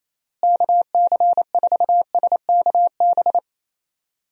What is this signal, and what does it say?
Morse code, heard as a keyed tone, about 28 wpm, 700 Hz. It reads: KC4SKB
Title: USA Call Signs - 6 Characters - Repeated 2x - 28wpm